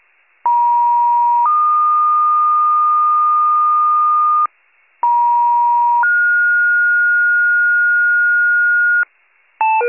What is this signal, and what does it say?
Two-tone page, heard as a one-second tone, then a three-second tone, about 950 then 1220 Hz, about 950 then 1430 Hz, with noise. It is heard from 0.4 to 9.0 s.
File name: pager / 83 RESCUE OFC / wcalert